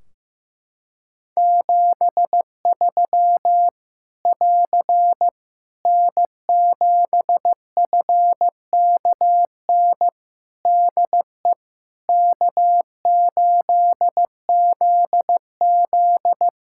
Morse code, heard as a keyed tone, about 15 wpm, 700 Hz. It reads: 73 <AR> N7FKN DE K8ZZ
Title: Practice QSO - Exchange 4 - Ending - 15wpm